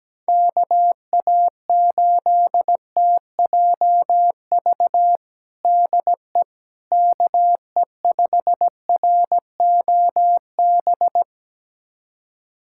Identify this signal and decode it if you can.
Morse code, heard as a keyed tone, about 17 wpm, 700 Hz. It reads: KA8TJV DE KE5ROB
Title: USA Callsign DE Callsign 17wpm